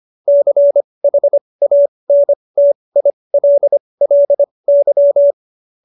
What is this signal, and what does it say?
Morse code, heard as a keyed tone, about 25 wpm, spaced as for 20 wpm, 570 Hz. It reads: CHANTILLY
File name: QSO316 20